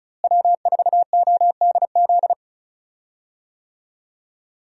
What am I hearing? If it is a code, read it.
Morse code, heard as a keyed tone, about 36 wpm, 700 Hz. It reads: W4ODZ
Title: USA Call Signs - 5 Characters - Repeated 3x - 35wpm